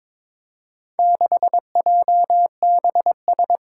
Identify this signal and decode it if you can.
Morse code, heard as a keyed tone, about 22 wpm, 700 Hz. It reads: 6JBS
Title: Four Challenging Letter-Number Combinations 22wpm